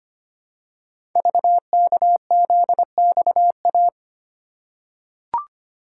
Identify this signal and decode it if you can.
Morse code, heard as a keyed tone, about 25 wpm, 700 Hz. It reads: VKZXA
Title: Code Groups - 5 Letters - 25wpm